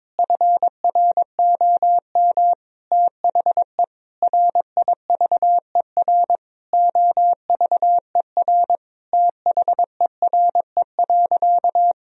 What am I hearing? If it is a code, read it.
Morse code, heard as a keyed tone, about 22 wpm, 700 Hz. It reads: FROM THE RIVER OVER THERE.
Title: Sentences from Top 1000 Words - Review Part 2 - 22wpm